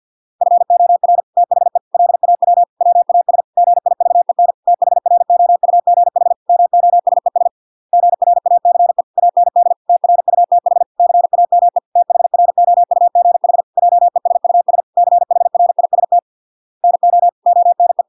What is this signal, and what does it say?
Morse code, heard as a keyed tone, about 50 wpm, 700 Hz. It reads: FOR THE LAW WAS GIVEN THROUGH MOSES GRACE AND TRUTH CAME THROUGH JESUS CHRIST NO ONE